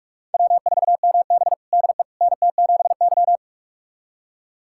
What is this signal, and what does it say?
Morse code, heard as a keyed tone, about 45 wpm, 700 Hz. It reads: W4MB DE NT7Y
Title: USA Callsign DE Callsign - 4 Characters - 45wpm